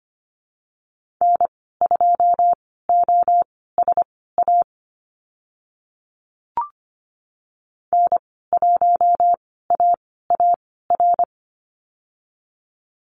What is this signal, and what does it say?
Morse code, heard as a keyed tone, about 25 wpm, spaced as for 16 wpm, 690 Hz. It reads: N2OSA N1AAR
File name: Lesson 15g - Number 5 - Call Signs 25wpm